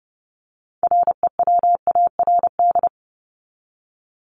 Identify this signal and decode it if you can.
Morse code, heard as a keyed tone, about 30 wpm, 700 Hz. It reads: REWARD
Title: Top 1751-2000 Words 30wpm